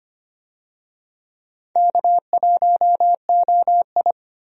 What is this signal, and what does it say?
Morse code, heard as a keyed tone, about 25 wpm, 700 Hz. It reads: K1OI
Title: USA Call Signs - 4 Characters - 25wpm